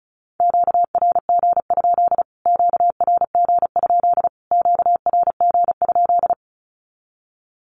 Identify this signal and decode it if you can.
Morse code, heard as a keyed tone, about 34 wpm, 700 Hz. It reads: QRG? QRG? QRG?
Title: Q-Codes Repeated 3x 35wpm